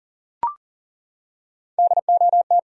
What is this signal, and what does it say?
Morse code, heard as a keyed tone, about 40 wpm, 700 Hz. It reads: DOT
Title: USA Common Names - Max 3 Letters - 40wpm